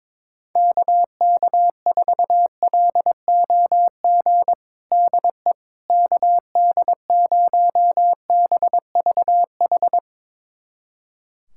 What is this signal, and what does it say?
Morse code, heard as a keyed tone, about 22 wpm, 700 Hz. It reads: KK4LOG DE KD0BVH
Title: USA Callsign DE Callsign - 6 Characters - 22wpm